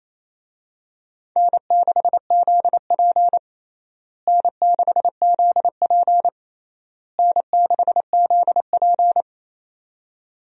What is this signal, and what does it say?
Morse code, heard as a keyed tone, about 28 wpm, 700 Hz. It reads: N6ZP N6ZP N6ZP
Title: USA Call Signs - 4 Characters - Repeated 3x - 28wpm